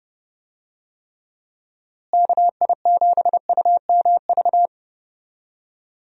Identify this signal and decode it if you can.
Morse code, heard as a keyed tone, about 30 wpm, 700 Hz. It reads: KI7UMV
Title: USA Call Signs - 6 Characters - Repeated 3x - 30wpm